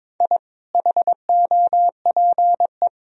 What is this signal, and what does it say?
Morse code, heard as a keyed tone, about 22 wpm, 700 Hz. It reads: I HOPE
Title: Sentences from Top 1000 Words - Review Part 4 - 22wpm